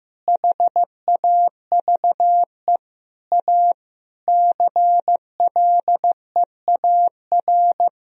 Morse code, heard as a keyed tone, about 15 wpm, 700 Hz. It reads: HAVE A CLEAR
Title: Sentences from Top 1000 Words - Review Part 2 - 15wpm